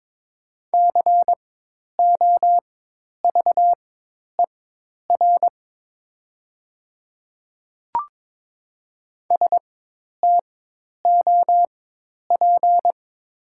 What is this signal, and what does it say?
Morse code, heard as a keyed tone, about 23 wpm, spaced as for 10 wpm, 700 Hz. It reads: COVER STOP
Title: 4x Character Spacing - Top 300 Words - 22wpm